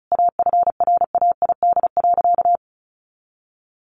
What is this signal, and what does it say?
Morse code, heard as a keyed tone, about 34 wpm, 700 Hz. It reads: AFRAID.
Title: Sentences from Top 1500 Words 35wpm